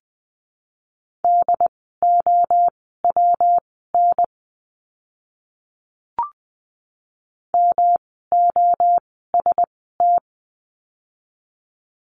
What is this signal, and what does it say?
Morse code, heard as a keyed tone, about 20 wpm, spaced as for 15 wpm, 700 Hz. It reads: DOWN MOST
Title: 237 Lesson 38f - Letter q - Words 20wpm